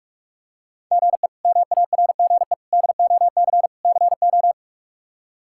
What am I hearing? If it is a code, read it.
Morse code, heard as a keyed tone, about 42 wpm, 700 Hz. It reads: GE MARGE DOC CO
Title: K1USN SST QSO Practice - 45wpm